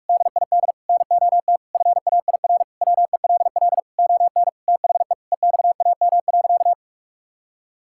Morse code, heard as a keyed tone, about 46 wpm, 690 Hz. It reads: DID NOT FAIR WELL ON THE EXAM.
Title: Sentences from Top 1000 Words - Review Part 5 - 45wpm